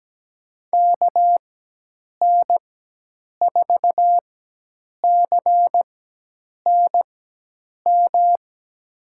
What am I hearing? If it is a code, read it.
Morse code, heard as a keyed tone, about 17 wpm, spaced as for 8 wpm, 700 Hz. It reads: KN4CNM
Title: USA Call Signs - 6 Characters - 4x Character Spacing - 17wpm